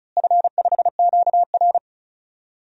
Morse code, heard as a keyed tone, about 34 wpm, 700 Hz. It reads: F5QR
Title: DXCC-100 Call Signs - 4 Characters - Repeated 2x - 35wpm